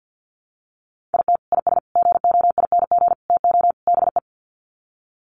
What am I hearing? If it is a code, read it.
Morse code, heard as a keyed tone, about 50 wpm, 700 Hz. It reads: IT IS GOING TO BE
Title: Sets of 5 Words 50wpm